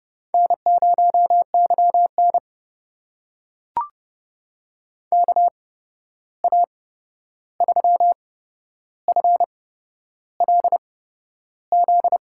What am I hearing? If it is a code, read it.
Morse code, heard as a keyed tone, about 30 wpm, spaced as for 11 wpm, 700 Hz. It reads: N0YN KA3FLZ
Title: USA Call Signs - 8x Character Spacing - 30wpm